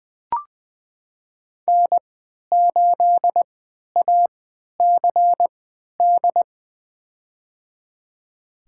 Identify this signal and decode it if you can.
Morse code, heard as a keyed tone, about 21 wpm, spaced as for 12 wpm, 700 Hz. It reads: N8ACD
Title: USA Call Signs - 5 Characters - 3x Character Spacing - 20wpm